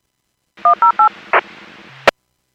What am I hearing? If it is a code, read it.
DTMF, heard as telephone touch tones, tones of about 110 ms. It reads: 108
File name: Businesses / 00316 23-55 Businesses 451.25 WPS